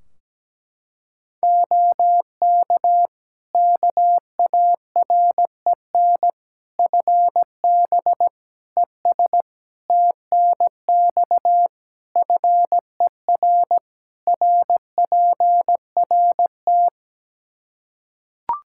Morse code, heard as a keyed tone, about 17 wpm, 700 Hz. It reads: OK KAREN FB ES TNX FER RPRT
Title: Practice QSO - Exchange 2 - Info - 17wpm